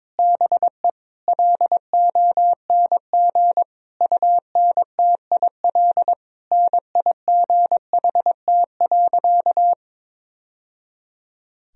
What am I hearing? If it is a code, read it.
Morse code, heard as a keyed tone, about 22 wpm, 700 Hz. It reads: BE LONG UNTIL NIGHT.
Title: Sentences from Top 1000 Words - Review Part 1 - 22wpm